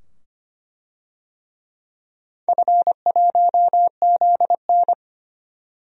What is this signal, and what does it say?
Morse code, heard as a keyed tone, about 25 wpm, 700 Hz. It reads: F1ZN
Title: DXCC-100 Call Signs - 4 Characters - 25wpm